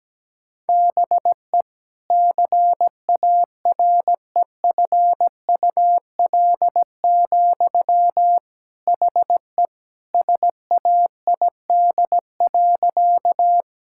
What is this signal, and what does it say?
Morse code, heard as a keyed tone, about 17 wpm, 700 Hz. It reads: BE CAREFUL, HE SAID.
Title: Sentences from Top 1600 Words 17wpm